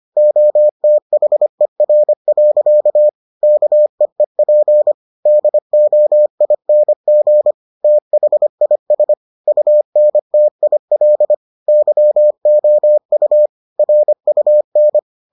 Morse code, heard as a keyed tone, about 25 wpm, 600 Hz. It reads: OTHER. KEEP DOING THIS UNTIL YOU RUN